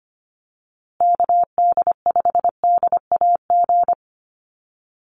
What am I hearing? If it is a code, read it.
Morse code, heard as a keyed tone, about 25 wpm, 700 Hz. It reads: KD5DAG
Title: USA Call Signs Repeated 2x 25wpm